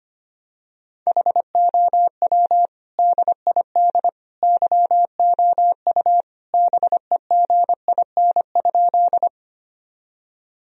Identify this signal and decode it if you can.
Morse code, heard as a keyed tone, about 25 wpm, 700 Hz. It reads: HOW DID YOU BEGIN?
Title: Sentences from Top 1000 Words - Review Part 1 - 25wpm